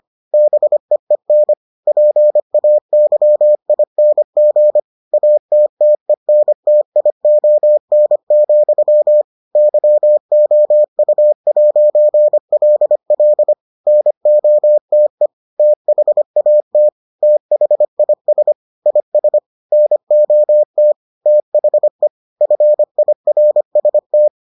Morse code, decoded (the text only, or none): BEEN PAYING ATTENTION, YOU'LL NOTE THAT THIS IS NOT THE FIRST